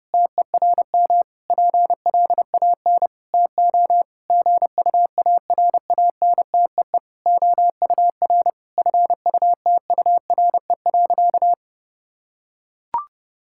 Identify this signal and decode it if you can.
Morse code, heard as a keyed tone, about 30 wpm, 700 Hz. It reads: TERM PLAN TO GUARANTEE OUR FUTURE.
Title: Sentences from Top 1600 Words 30wpm